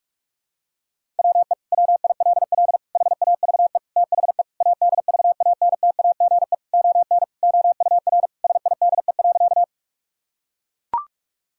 Morse code, heard as a keyed tone, about 42 wpm, 700 Hz. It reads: WE WILL HAVE THE ADVANTAGE ON OUR SIDE.
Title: Sentences from Top 1000 Words - Review Part 6 - 45wpm